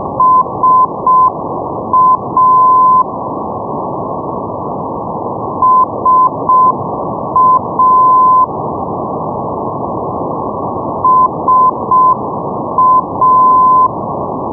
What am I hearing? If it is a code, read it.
Morse code, heard as a keyed tone, about 6 wpm, 1020 Hz, with noise. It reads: SA SA SA
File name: europe / pol / pol-531 ndb darlowo sa 474.5 (rug) 2008 rb